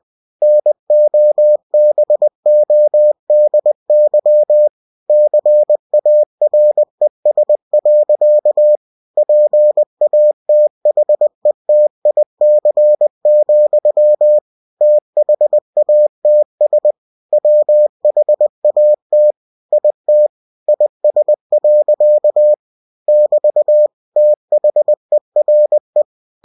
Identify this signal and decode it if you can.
Morse code, heard as a keyed tone, about 20 wpm, 600 Hz. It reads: NOBODY CARES. PATHETIC, THATS WHAT IT IS. <BT> THERE
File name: cw-books / winnie-the-pooh / winnie-20wpm-page-0044